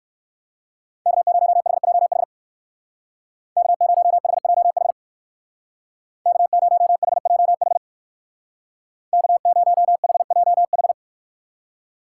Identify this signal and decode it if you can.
Morse code, heard as a keyed tone, about 56 wpm, 700 Hz. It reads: K0HJH K0HJH K0HJH K0HJH
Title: Speed Racing - USA 1x3 Call Signs 45wpm